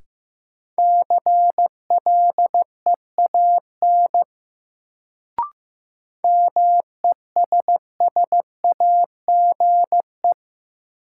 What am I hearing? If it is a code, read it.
Morse code, heard as a keyed tone, about 15 wpm, 700 Hz. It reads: CLEAN MESSAGE
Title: Sentences from Top 1100 Words - New Word Drill Alt - 15wpm